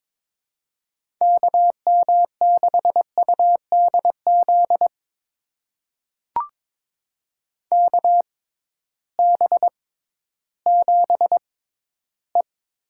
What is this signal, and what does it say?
Morse code, heard as a keyed tone, about 22 wpm, spaced as for 13 wpm, 700 Hz. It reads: KM6UDZ KB7E
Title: USA Call Signs - 6 Characters - 6x Character Spacing - 22wpm